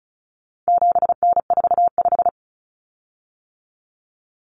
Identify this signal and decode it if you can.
Morse code, heard as a keyed tone, about 34 wpm, 700 Hz. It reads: 7N45
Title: Common Cut Numbers - 4-Digit 35wpm